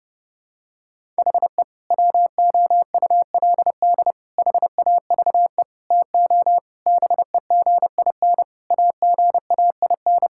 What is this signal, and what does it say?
Morse code, heard as a keyed tone, about 30 wpm, 700 Hz. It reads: HE WOULD HAVE TO BEGIN AGAIN
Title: Sentences from Top 1000 Words - Review Part 5 - 30wpm